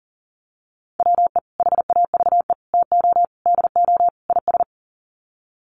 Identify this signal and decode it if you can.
Morse code, heard as a keyed tone, about 40 wpm, 700 Hz. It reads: WE HAVE TO DO IS
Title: Sets of 5 Words Encore 40wpm